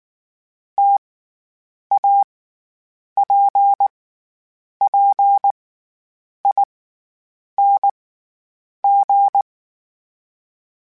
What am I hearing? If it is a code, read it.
Morse code, heard as a keyed tone, about 19 wpm, spaced as for 8 wpm, 800 Hz. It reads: TAPPING
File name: Variable Speed / Variable Speed-mp3s / T AOEI's sp @18